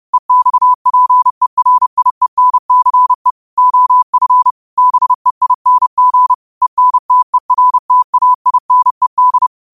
Morse code, heard as a keyed tone, about 31 wpm, 1000 Hz. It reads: EXPERIENCE OF BEING ENTERTAINED
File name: qotdfiles / 30wpm